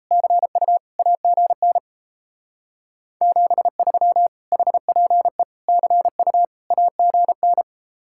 Morse code, heard as a keyed tone, about 34 wpm, 700 Hz. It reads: CU AGN 73 HPE CU AGN
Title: Speed-Racing Common QSO Phrases - 28wpm